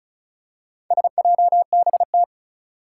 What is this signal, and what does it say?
Morse code, heard as a keyed tone, about 36 wpm, 700 Hz. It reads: SJBT
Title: Four Challenging Letter-Number Combinations 35wpm